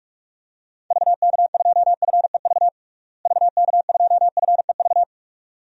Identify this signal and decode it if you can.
Morse code, heard as a keyed tone, about 46 wpm, 700 Hz. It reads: VK2FEV VK2FEV
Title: DXCC-100 Call Signs - 6 Characters - Repeated 2x - 45wpm